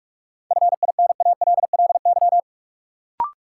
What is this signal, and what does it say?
Morse code, heard as a keyed tone, about 42 wpm, 700 Hz. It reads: FINALLY
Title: Sentences from Top 1000 Words - New Word Drill - 45wpm